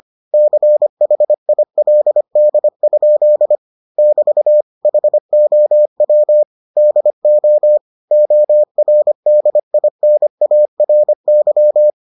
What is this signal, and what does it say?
Morse code, heard as a keyed tone, about 25 wpm, 600 Hz. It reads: CHILD? <BT> HOW DO ORDINARY